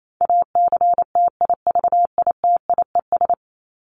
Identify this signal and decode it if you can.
Morse code, heard as a keyed tone, about 28 wpm, 700 Hz. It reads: ACTIVITIES
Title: ES:S Suffix - Top 100 Words with Base Words 28wpm